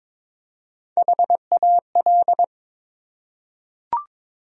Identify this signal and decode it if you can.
Morse code, heard as a keyed tone, about 22 wpm, 700 Hz. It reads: HAL
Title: USA Common Names - Max 3 Letters - 22wpm